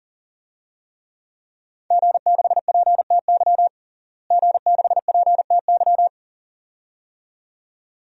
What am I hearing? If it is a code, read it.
Morse code, heard as a keyed tone, about 40 wpm, 700 Hz. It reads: G6PTY G6PTY
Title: DXCC-100 Call Signs - 5 Characters - Repeated 2x - 40wpm